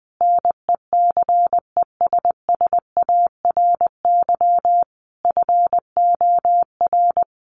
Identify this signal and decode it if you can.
Morse code, heard as a keyed tone, about 20 wpm, 700 Hz. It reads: NECESSARY FOR
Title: Sentences from Top 1500 Words 20wpm